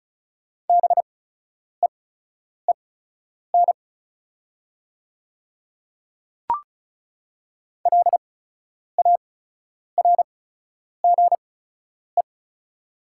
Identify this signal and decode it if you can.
Morse code, heard as a keyed tone, about 34 wpm, spaced as for 10 wpm, 700 Hz. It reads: BEEN LARGE
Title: 8x Character Spacing - Top 200 Words - 35wpm